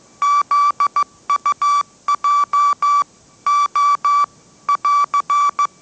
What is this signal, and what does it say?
Morse code, heard as a keyed tone, about 15 wpm, 1160 Hz. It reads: ZUJ O <AR>
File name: mrx / mrx066 16wpm